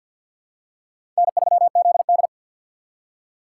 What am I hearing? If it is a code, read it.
Morse code, heard as a keyed tone, about 46 wpm, 700 Hz. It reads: N3ZD
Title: USA Call Signs - 4 Characters - 3x Character Spacing - 50wpm